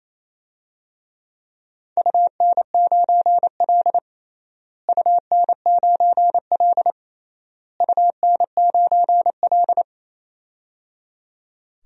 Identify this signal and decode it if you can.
Morse code, heard as a keyed tone, about 28 wpm, 700 Hz. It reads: UN9L UN9L UN9L
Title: DXCC-100 Call Signs - 4 Characters - Repeated 3x - 28wpm